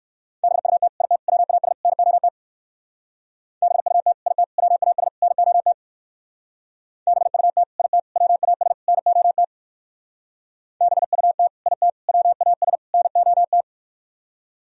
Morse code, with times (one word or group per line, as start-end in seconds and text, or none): 0.4-0.9 BUT
1.0-1.2 IT
1.3-1.7 WAS
1.9-2.3 NOT
3.6-4.1 BUT
4.3-4.4 IT
4.6-5.1 WAS
5.2-5.7 NOT
7.1-7.6 BUT
7.8-8.0 IT
8.2-8.7 WAS
8.9-9.4 NOT
10.8-11.5 BUT
11.7-11.9 IT
12.1-12.8 WAS
12.9-13.6 NOT